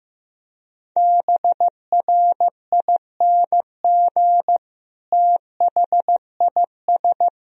Morse code, heard as a keyed tone, about 15 wpm, 700 Hz. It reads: BRING THIS
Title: Sentences from Top 500 Words and Abbreviations 15wpm